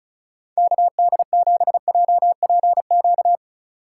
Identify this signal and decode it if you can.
Morse code, heard as a keyed tone, about 36 wpm, 700 Hz. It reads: KD7JPQ
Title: Speed Racing - USA 2x3 Call Signs 30wpm